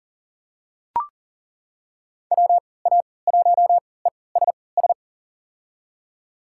Morse code, heard as a keyed tone, about 40 wpm, spaced as for 23 wpm, 700 Hz. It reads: WA1ESS
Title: USA Call Signs - 6 Characters - 3x Character Spacing - 40wpm